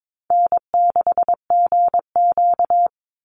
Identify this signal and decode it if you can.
Morse code, heard as a keyed tone, about 22 wpm, 700 Hz. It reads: N6GQ
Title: Easy Call Signs Repeated 2x - 22wpm